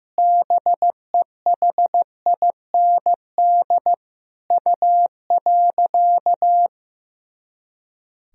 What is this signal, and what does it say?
Morse code, heard as a keyed tone, about 15 wpm, 700 Hz. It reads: BEHIND U.
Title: Sentences from Top 500 Words and Abbreviations 15wpm